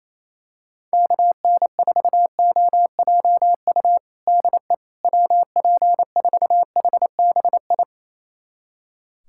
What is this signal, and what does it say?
Morse code, heard as a keyed tone, about 28 wpm, 700 Hz. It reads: KN4OJU DE WP4HBI
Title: USA Callsign DE Callsign 28wpm